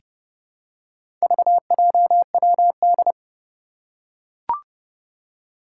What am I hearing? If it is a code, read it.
Morse code, heard as a keyed tone, about 30 wpm, 700 Hz. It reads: VJWD
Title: Four Challenging Letter-Number Combinations 30wpm